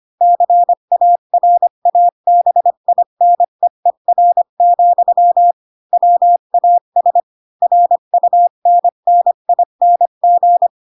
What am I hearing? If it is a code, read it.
Morse code, heard as a keyed tone, about 25 wpm, 700 Hz. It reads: CARABINEER, WAS RUNNING
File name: Pinocchio / Pinocchio ch4 20 25 ik1hge